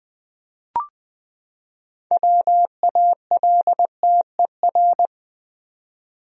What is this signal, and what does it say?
Morse code, heard as a keyed tone, about 20 wpm, 700 Hz. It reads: WALTER
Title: USA Common Names - 20wpm